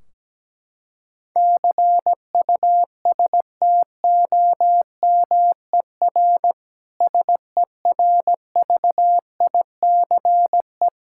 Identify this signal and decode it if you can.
Morse code, heard as a keyed tone, about 17 wpm, 700 Hz. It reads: CUSTOMER SERVICE
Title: Practice QSO - Exchange 3 - More Info - 17wpm